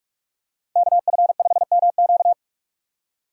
Morse code, heard as a keyed tone, about 44 wpm, 700 Hz. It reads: KF5MQ